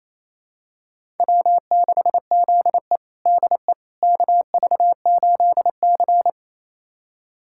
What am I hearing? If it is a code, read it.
Morse code, heard as a keyed tone, about 28 wpm, 700 Hz. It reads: W6ZE DE KV8C
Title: USA Callsign DE Callsign - 4 Characters - 28wpm